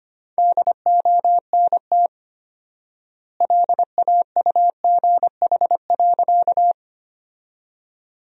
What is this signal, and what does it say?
Morse code, heard as a keyed tone, about 25 wpm, 700 Hz. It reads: DONT LAUGH.
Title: 4x Word-Spacing - Sentences from Top 500 Words - 25wpm